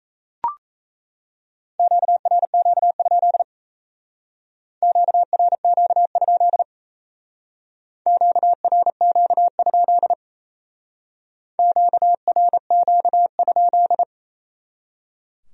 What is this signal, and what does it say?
Morse code, heard as a keyed tone, about 35 wpm, 700 Hz. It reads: QRQ? QRQ? QRQ? QRQ?